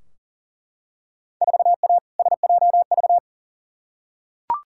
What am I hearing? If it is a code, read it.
Morse code, heard as a keyed tone, about 37 wpm, 700 Hz. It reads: 4A SJV
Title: Field Day Exchanges Encore - 40wpm